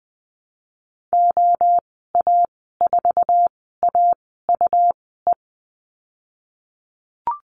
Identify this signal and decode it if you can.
Morse code, heard as a keyed tone, about 20 wpm, spaced as for 15 wpm, 700 Hz. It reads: OA4AUE
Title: 154 Lesson 25g - Letter p - Call Signs 20wpm